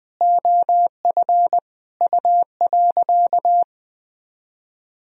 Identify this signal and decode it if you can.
Morse code, heard as a keyed tone, about 20 wpm, 700 Hz. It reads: OF U.
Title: Sentences from Top 200 Words and Abbreviations 20wpm